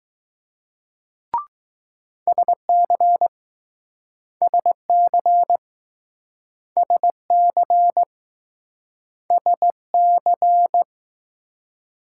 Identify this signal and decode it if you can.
Morse code, heard as a keyed tone, about 20 wpm, 700 Hz. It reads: SC SC SC SC